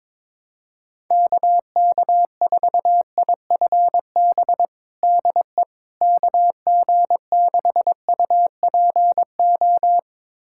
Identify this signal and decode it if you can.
Morse code, heard as a keyed tone, about 22 wpm, 700 Hz. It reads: KK4IFB DE KG6UPO
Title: USA Callsign DE Callsign - 6 Characters - 22wpm